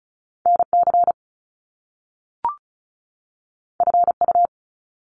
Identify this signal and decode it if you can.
Morse code, heard as a keyed tone, about 36 wpm, 700 Hz. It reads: NC FU
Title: Top Two Letter Combinations in Words 35wpm